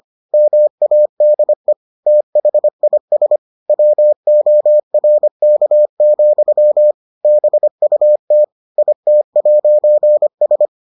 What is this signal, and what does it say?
Morse code, heard as a keyed tone, about 25 wpm, 600 Hz. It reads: MADE THIS WORK, BUT IT'S